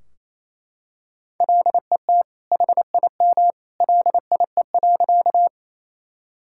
Morse code, heard as a keyed tone, about 27 wpm, 700 Hz. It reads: LET HIM LIE.
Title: Sentences from Top 1600 Words 28wpm